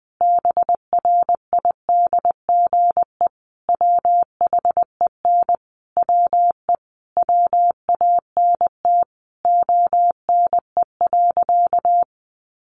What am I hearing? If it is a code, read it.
Morse code, heard as a keyed tone, about 20 wpm, 700 Hz. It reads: BRIDGE WHEN WE WANT ONE.
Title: Sentences from Top 1200 Words 20wpm